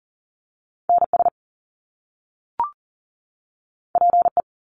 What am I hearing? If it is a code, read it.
Morse code, heard as a keyed tone, about 40 wpm, 700 Hz. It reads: NS PE
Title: Canadian Provinces and Territories 40wpm